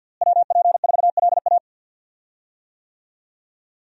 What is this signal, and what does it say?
Morse code, heard as a keyed tone, about 52 wpm, 700 Hz. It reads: WP4LA
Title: USA Call Signs - 5 Characters - Repeated 3x - 50wpm